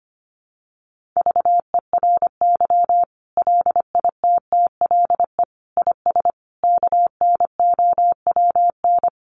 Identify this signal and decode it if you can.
Morse code, heard as a keyed tone, about 25 wpm, 700 Hz. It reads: VERY LITTLE IS KNOWN
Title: Sentences from Top 1500 Words 25wpm